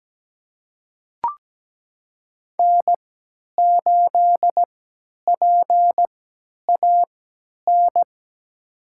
Morse code, heard as a keyed tone, about 17 wpm, spaced as for 10 wpm, 700 Hz. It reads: N8PAN
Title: USA Call Signs - 5 Characters - 3x Character Spacing - 17wpm